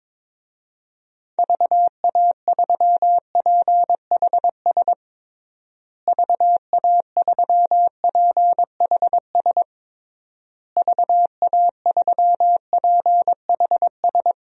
Morse code, heard as a keyed tone, about 22 wpm, 700 Hz. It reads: VA3PHS VA3PHS VA3PHS
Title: DXCC-100 Call Signs - 6 Characters - Repeated 3x - 22wpm